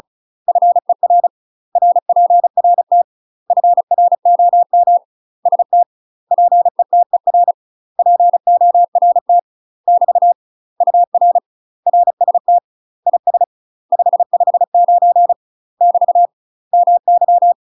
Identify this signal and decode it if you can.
Morse code, heard as a keyed tone, about 36 wpm, 700 Hz. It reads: FER RPRT FROM ST PETER PORT <BT> UR RST IS 559 <BT> MY
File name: UK MASTER QSOs TRAINING / 35WPM / UK MASTER QSOs 0041